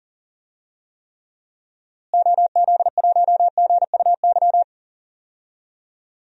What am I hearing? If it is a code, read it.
Morse code, heard as a keyed tone, about 40 wpm, 700 Hz. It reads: OZ1GUY
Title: DXCC-100 Call Signs - 6 Characters - Repeated 3x - 40wpm